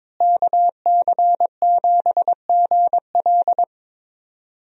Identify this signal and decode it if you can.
Morse code, heard as a keyed tone, about 22 wpm, 700 Hz. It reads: KC7GL
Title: USA Call Signs - 5 Characters - Repeated 2x - 22wpm